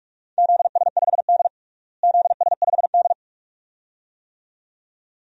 Morse code, heard as a keyed tone, about 46 wpm, 700 Hz. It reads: ZS5D ZS5D
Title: DXCC-100 Call Signs - 4 Characters - Repeated 2x - 45wpm